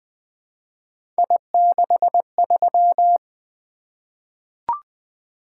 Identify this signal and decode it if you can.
Morse code, heard as a keyed tone, about 20 wpm, 700 Hz. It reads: I63
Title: Three Letter-Number Combinations - 20wpm